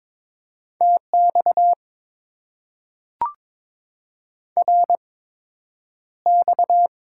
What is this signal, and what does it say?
Morse code, heard as a keyed tone, about 22 wpm, spaced as for 9 wpm, 700 Hz. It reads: TX RX
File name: Top 90 QSO Elements - 8x Character Spacing - 22wpm